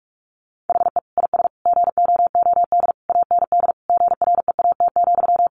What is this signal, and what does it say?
Morse code, heard as a keyed tone, about 44 wpm, 700 Hz. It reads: HE IS GOOD AND GREAT,
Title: Sentences from Top 1400 Words 45wpm